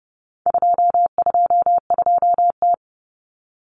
Morse code, heard as a keyed tone, about 30 wpm, 700 Hz. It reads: 222T
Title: Common Cut Numbers - 4-Digit 30wpm